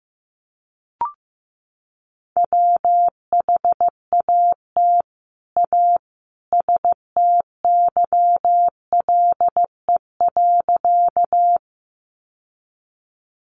Morse code, heard as a keyed tone, about 15 wpm, 700 Hz. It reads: WHAT A STYLE.
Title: Sentences from Top 800 Words 15wpm